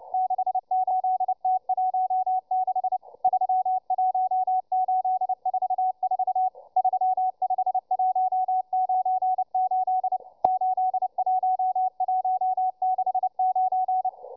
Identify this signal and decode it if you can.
Morse code, heard as a keyed tone, about 29 wpm, 750 Hz, with radio noise. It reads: 68T16 31844 35198 81169